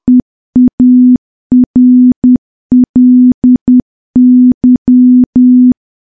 Morse code, heard as a keyed tone, about 10 wpm, 250 Hz. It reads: EARLY
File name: morse / morse-training-10WPM